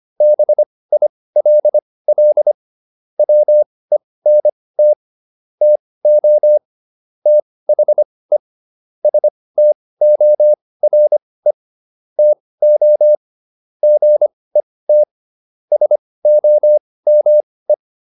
Morse code, heard as a keyed tone, about 25 wpm, spaced as for 18 wpm, 600 Hz. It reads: BILL WENT TO THE STORE TO GET SOME